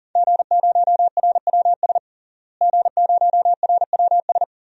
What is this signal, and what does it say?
Morse code, heard as a keyed tone, about 40 wpm, 700 Hz. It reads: G0RWS G0RWS
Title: DXCC-100 Call Signs - 5 Characters - Repeated 2x - 40wpm